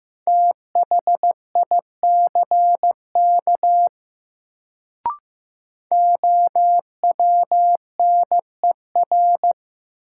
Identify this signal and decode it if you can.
Morse code, heard as a keyed tone, about 15 wpm, 700 Hz. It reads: THICK OWNER
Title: Sentences from Top 1600 Words - New Word Drill - 15wpm